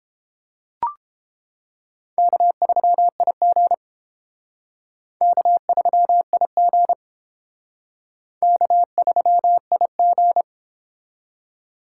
Morse code, heard as a keyed tone, about 30 wpm, 700 Hz. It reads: K3IG K3IG K3IG